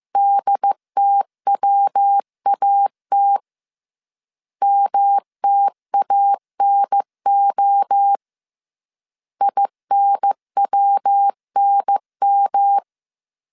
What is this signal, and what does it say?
Morse code, heard as a keyed tone, about 15 wpm, 800 Hz. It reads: DTWAT MTANO INWNM